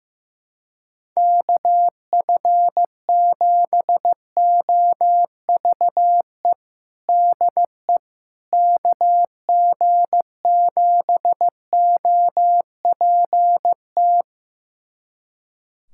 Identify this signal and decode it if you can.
Morse code, heard as a keyed tone, about 15 wpm, 700 Hz. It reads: KF7OVE DE KG7OPT
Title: USA Callsign DE Callsign - 6 Characters - 15wpm